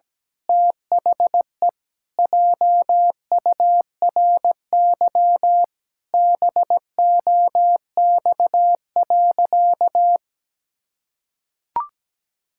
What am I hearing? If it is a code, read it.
Morse code, heard as a keyed tone, about 17 wpm, 700 Hz. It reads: THE JURY BOX.
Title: Pangram Challenge v2 - 17wpm